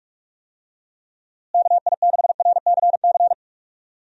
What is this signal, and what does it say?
Morse code, heard as a keyed tone, about 41 wpm, 700 Hz. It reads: KI6RCC
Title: USA Call Signs - 6 Characters - Repeated 2x - 45wpm